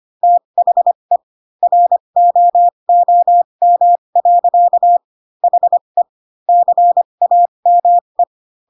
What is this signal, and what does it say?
Morse code, heard as a keyed tone, about 25 wpm, spaced as for 22 wpm, 700 Hz. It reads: THE ROOM. HE CAME
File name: Pinocchio / Pinocchio ch3 18 25 ik1hge